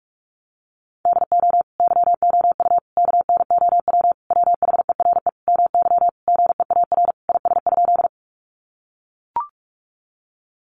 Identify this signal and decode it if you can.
Morse code, heard as a keyed tone, about 44 wpm, 700 Hz. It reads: DO YOU KNOW WHERE MY GEAR IS?
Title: Sentences from Top 1400 Words 45wpm